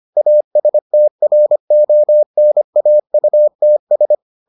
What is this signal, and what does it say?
Morse code, heard as a keyed tone, about 25 wpm, 600 Hz. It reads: ASTRONAUTS